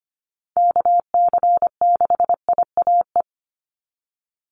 Morse code, heard as a keyed tone, about 25 wpm, 700 Hz. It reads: KC6IAE
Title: USA Call Signs 25wpm